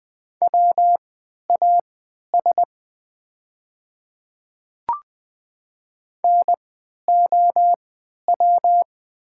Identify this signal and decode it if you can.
Morse code, heard as a keyed tone, about 20 wpm, spaced as for 11 wpm, 690 Hz. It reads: WAS NOW
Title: 3x Character Spacing - Top 100 Words - 20wpm